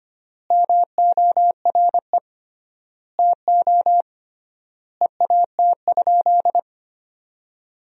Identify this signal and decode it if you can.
Morse code, heard as a keyed tone, about 25 wpm, 700 Hz. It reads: MORE TO EAT?
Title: 3x Word-Spacing - Sentences from Top 300 Words - 25wpm